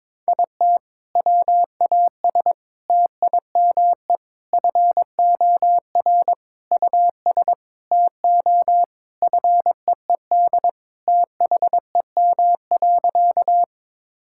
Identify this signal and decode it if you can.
Morse code, heard as a keyed tone, about 22 wpm, 700 Hz. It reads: IT WAS TIME FOR US TO FEED THEM.
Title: Sentences from Top 1600 Words 22wpm